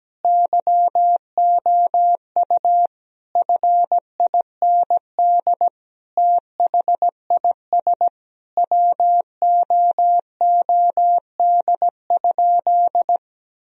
Morse code, heard as a keyed tone, about 17 wpm, 700 Hz. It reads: YOU FIND THIS WOOD?
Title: Sentences from Top 1000 Words - Review Part 2 - 17wpm